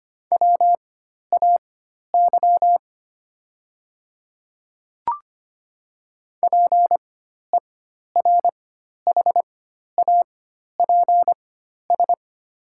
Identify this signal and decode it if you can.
Morse code, heard as a keyed tone, about 25 wpm, spaced as for 12 wpm, 700 Hz. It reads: WAY PERHAPS
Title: 4x Character Spacing - Top 500 Words - 25wpm